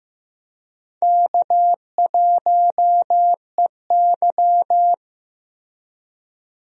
Morse code, heard as a keyed tone, about 15 wpm, 700 Hz. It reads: K1EY
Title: USA Call Signs - 4 Characters - Repeated 2x - 15wpm